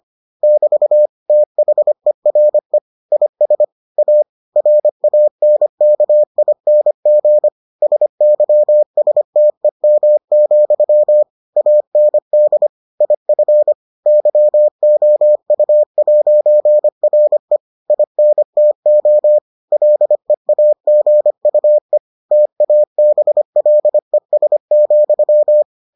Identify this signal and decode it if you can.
Morse code, heard as a keyed tone, about 25 wpm, 600 Hz. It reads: <BT> THERE IS A RANKING SYSTEM, AND IF YOU'RE INTO LEAGUE TABLES,